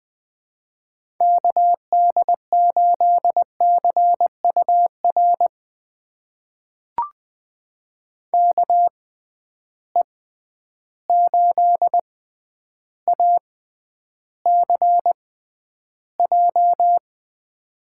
Morse code, heard as a keyed tone, about 21 wpm, spaced as for 10 wpm, 700 Hz. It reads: KD8CUR KE8ACJ
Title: USA Call Signs - 6 Characters - 6x Character Spacing - 20wpm